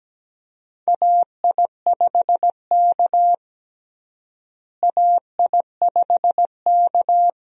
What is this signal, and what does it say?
Morse code, heard as a keyed tone, about 17 wpm, 700 Hz. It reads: AI5K AI5K
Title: USA Call Signs - 4 Characters - Repeated 2x - 17wpm